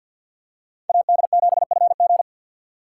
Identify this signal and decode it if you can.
Morse code, heard as a keyed tone, about 53 wpm, 700 Hz. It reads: AD7FG
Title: USA Call Signs - 5 Characters - 8x Character Spacing - 50wpm